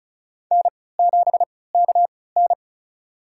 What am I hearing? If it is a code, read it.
Morse code, heard as a keyed tone, about 36 wpm, spaced as for 20 wpm, 700 Hz. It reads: N7KN
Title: USA Call Signs - 4 Characters - 3x Character Spacing - 35wpm